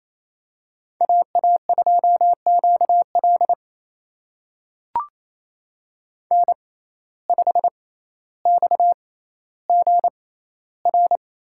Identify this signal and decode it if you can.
Morse code, heard as a keyed tone, about 28 wpm, spaced as for 14 wpm, 700 Hz. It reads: AA2QL N5XGR
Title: USA Call Signs - 5 Characters - 6x Character Spacing - 28wpm